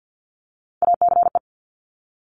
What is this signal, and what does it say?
Morse code, heard as a keyed tone, about 50 wpm, 700 Hz. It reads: ACE
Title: Top Three Letter Combinations in Words 50wpm